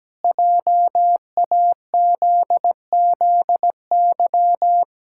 Morse code, heard as a keyed tone, about 17 wpm, 700 Hz. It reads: JAZZY
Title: Pangram Challenge v2 - 17wpm